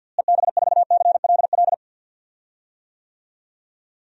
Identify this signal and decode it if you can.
Morse code, heard as a keyed tone, about 52 wpm, 700 Hz. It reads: EB4CLL
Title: DXCC-100 Call Signs - 6 Characters - Repeated 3x - 50wpm